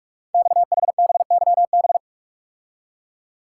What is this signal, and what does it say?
Morse code, heard as a keyed tone, about 44 wpm, 700 Hz. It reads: XHBYB
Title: Code Groups - 5 Letters - 45wpm